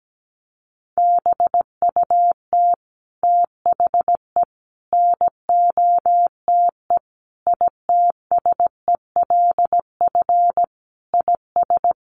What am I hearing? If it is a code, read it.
Morse code, heard as a keyed tone, about 17 wpm, 700 Hz. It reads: BUT THE NOTE ITSELF IS
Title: Sentences from Top 800 Words 17wpm